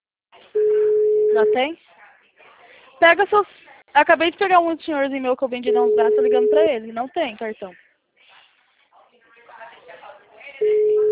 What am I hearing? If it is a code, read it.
Ringback tone (Europe, 425 Hz), with speech.
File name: out- -2026- - - .11689